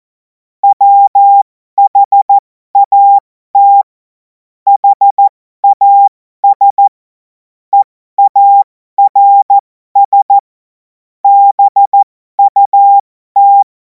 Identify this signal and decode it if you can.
Morse code, heard as a keyed tone, about 14 wpm, spaced as for 12 wpm, 800 Hz. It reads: WHAT HAS EARS BUT
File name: Morsekurs / Witze